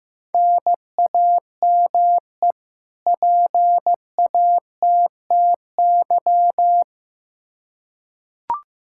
Morse code, heard as a keyed tone, about 16 wpm, 700 Hz. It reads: NAME PATTY
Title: K1USN SST QSO Practice 15wpm